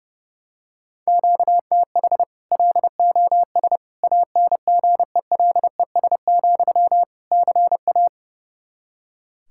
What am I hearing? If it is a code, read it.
Morse code, heard as a keyed tone, about 32 wpm, 700 Hz. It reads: QTH LOS ANGELES, CA
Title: Practice QSO - Exchange 1 - Beginning - 30wpm